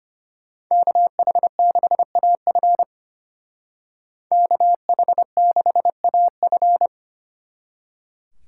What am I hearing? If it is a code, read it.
Morse code, heard as a keyed tone, about 33 wpm, 700 Hz. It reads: KH6AF KH6AF